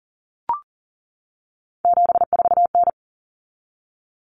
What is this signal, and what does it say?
Morse code, heard as a keyed tone, about 37 wpm, 700 Hz. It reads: 74N
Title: Common Cut Numbers - 3-Digit 40wpm